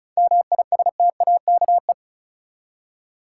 Morse code, heard as a keyed tone, about 36 wpm, 700 Hz. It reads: MISTAKE
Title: Seven Letter Words - 35wpm